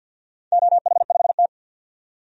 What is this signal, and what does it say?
Morse code, heard as a keyed tone, about 47 wpm, 700 Hz. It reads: OH5T
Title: DXCC-100 Call Signs - 4 Characters - 50wpm